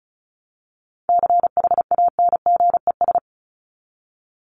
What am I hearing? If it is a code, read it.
Morse code, heard as a keyed tone, about 34 wpm, 700 Hz. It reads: CHANGES
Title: ES:S Suffix - Top 100 Words with Base Words 35wpm